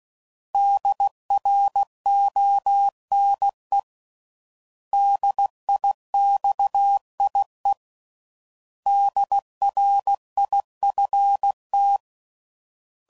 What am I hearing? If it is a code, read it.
Morse code, heard as a keyed tone, about 16 wpm, 800 Hz. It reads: DRONE DIXIE DRIFT
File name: Prefixes / By First Letter / Ds